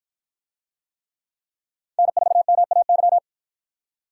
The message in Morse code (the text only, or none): N4MAY